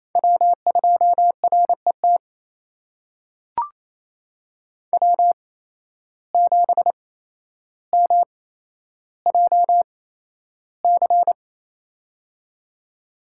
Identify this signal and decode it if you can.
Morse code, heard as a keyed tone, about 28 wpm, spaced as for 12 wpm, 700 Hz. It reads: W2RET W7MJC
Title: USA Call Signs - 5 Characters - 8x Character Spacing - 28wpm